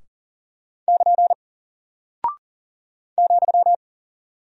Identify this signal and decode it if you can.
Morse code, heard as a keyed tone, about 38 wpm, 700 Hz. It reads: <KN> ,
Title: Single Character (Letter:Number:Punctuation:Prosign) Alt 40wpm